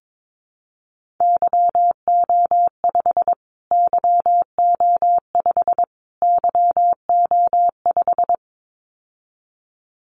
Morse code, heard as a keyed tone, about 22 wpm, 700 Hz. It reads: YO5 YO5 YO5
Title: Callsign Prefixes for Top 25 DXCC - Repeated 3x - 22wpm